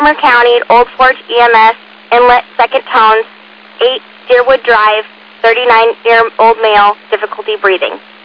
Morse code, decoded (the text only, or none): none